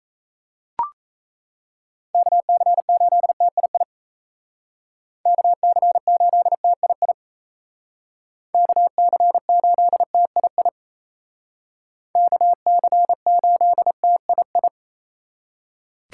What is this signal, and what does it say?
Morse code, heard as a keyed tone, about 36 wpm, 700 Hz. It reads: KC8TII KC8TII KC8TII KC8TII